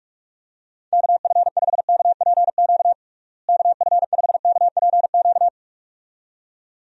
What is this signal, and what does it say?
Morse code, heard as a keyed tone, about 46 wpm, 700 Hz. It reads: KF5KPQ KF5KPQ
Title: USA Call Signs - 6 Characters - Repeated 2x - 45wpm